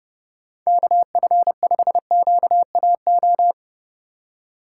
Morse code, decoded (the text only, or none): KF5QAO